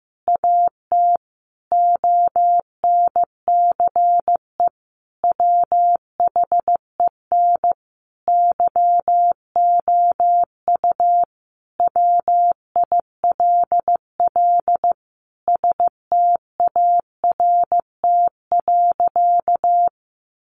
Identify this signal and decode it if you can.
Morse code, heard as a keyed tone, about 15 wpm, 700 Hz. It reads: AT ONCE WHEN YOU WILL START.
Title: Sentences from Top 300 Words 15wpm